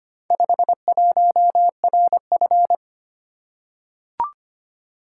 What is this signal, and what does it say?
Morse code, heard as a keyed tone, about 25 wpm, 700 Hz. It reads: 51RF
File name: practice / 25wpm / Four Letter-Number Combinations - 25wpm